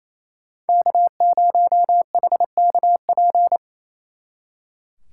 Morse code, heard as a keyed tone, about 33 wpm, 700 Hz. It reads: K0HKP
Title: Speed Racing - USA Call Signs 28wpm